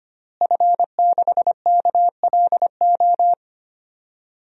Morse code, heard as a keyed tone, about 25 wpm, 700 Hz. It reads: F6KLO
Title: DXCC-100 Call Signs - 5 Characters - Repeated 3x - 25wpm